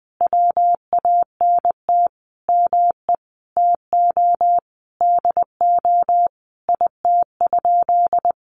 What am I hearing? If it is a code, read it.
Morse code, heard as a keyed tone, about 20 wpm, 700 Hz. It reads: WANT ME TO DO IT?
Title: Sentences from Top 200 Words 20wpm